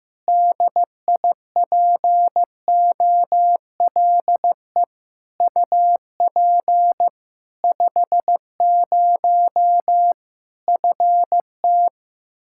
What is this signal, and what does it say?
Morse code, heard as a keyed tone, about 15 wpm, 700 Hz. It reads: DIPOLE UP 50 FT
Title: Practice QSO - Exchange 2 - Info - 15wpm